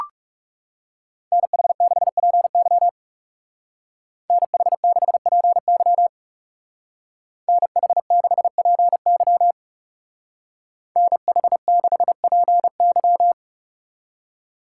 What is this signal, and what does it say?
Morse code, heard as a keyed tone, about 35 wpm, 700 Hz. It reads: NH6PY NH6PY NH6PY NH6PY
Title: Speed Racing - USA 2x2 Call Signs 30wpm